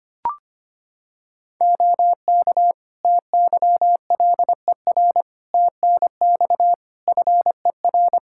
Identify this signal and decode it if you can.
Morse code, heard as a keyed tone, about 25 wpm, 700 Hz. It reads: OK TYLER TNX FER
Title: Practice QSO - Exchange 4 - Ending - 25wpm